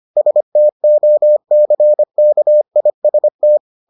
Morse code, heard as a keyed tone, about 25 wpm, 600 Hz. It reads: STOCKIST